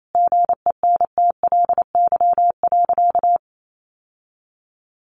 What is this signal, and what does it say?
Morse code, heard as a keyed tone, about 28 wpm, 700 Hz. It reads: GENTLY.
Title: Sentences from Top 1500 Words 28wpm